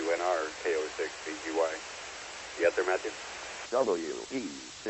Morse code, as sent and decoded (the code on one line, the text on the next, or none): none